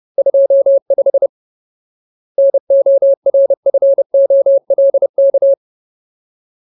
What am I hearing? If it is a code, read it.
Morse code, heard as a keyed tone, about 30 wpm, 550 Hz. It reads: 25 NORFOLK
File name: Ultimate Word List / 30WPM Ultimate